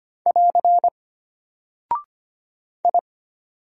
Morse code, heard as a keyed tone, about 25 wpm, 700 Hz. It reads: <AR> I
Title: Single Character (Letter:Number:Punctuation:Prosign) Alt 25wpm